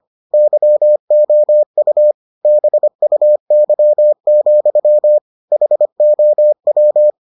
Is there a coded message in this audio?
Yes: Morse code from 0.3 to 7.1 s, heard as a keyed tone, about 25 wpm, 600 Hz. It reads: YOU BUY, HOW